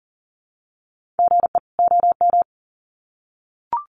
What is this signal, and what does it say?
Morse code, heard as a keyed tone, about 40 wpm, 700 Hz. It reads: GE OM
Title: Common QSO Phrases 40wpm